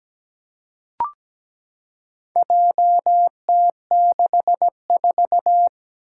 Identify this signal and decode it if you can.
Morse code, heard as a keyed tone, about 17 wpm, 700 Hz. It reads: JT64
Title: Four Challenging Letter-Number Combinations 17wpm